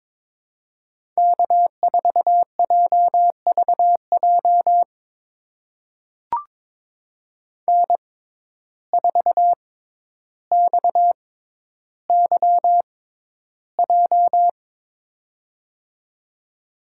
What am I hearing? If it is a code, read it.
Morse code, heard as a keyed tone, about 23 wpm, spaced as for 11 wpm, 700 Hz. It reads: K4JVJ N4XYJ
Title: USA Call Signs - 6x Character Spacing - 22wpm